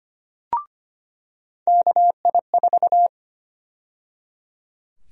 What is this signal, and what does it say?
Morse code, heard as a keyed tone, about 25 wpm, 700 Hz. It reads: KI4
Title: USA Call Sign Prefixes - 25wpm